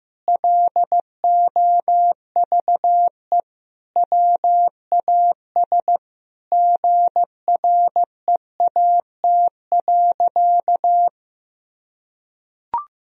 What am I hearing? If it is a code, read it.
Morse code, heard as a keyed tone, about 15 wpm, 700 Hz. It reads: LOVE WAS GREAT.
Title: Sentences from Top 1000 Words - Review Part 1 - 15wpm